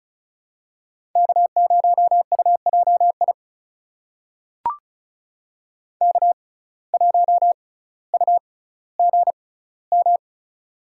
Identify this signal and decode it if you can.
Morse code, heard as a keyed tone, about 34 wpm, spaced as for 18 wpm, 700 Hz. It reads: K0UJI K1UGM
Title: USA Call Signs - 5 Characters - 6x Character Spacing - 35wpm